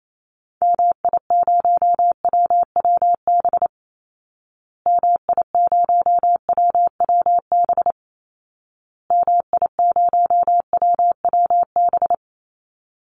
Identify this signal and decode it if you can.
Morse code, heard as a keyed tone, about 28 wpm, 700 Hz. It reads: MI0WWB MI0WWB MI0WWB
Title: Call Signs Repeated 2x - 28wpm